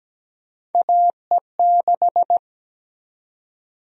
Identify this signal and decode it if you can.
Morse code, heard as a keyed tone, about 17 wpm, 700 Hz. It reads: AE6
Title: USA Call Sign Prefixes - 17wpm